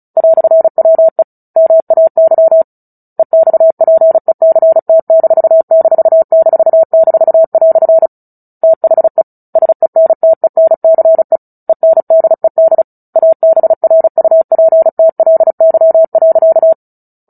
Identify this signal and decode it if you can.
Morse code, heard as a keyed tone, about 34 wpm, 650 Hz. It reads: "WE MAY EXPECT----" THE SENTENCE ENDED ABRUPTLY.
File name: Book Two / 09-Chapter 9 - 35 WPM